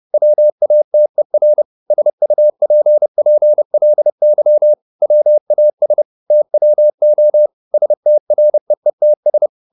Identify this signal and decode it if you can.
Morse code, heard as a keyed tone, about 30 wpm, 600 Hz. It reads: WATER SUPPLY WAS TWO STREETS
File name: the old man and the sea / the old man and the sea 1 30wpm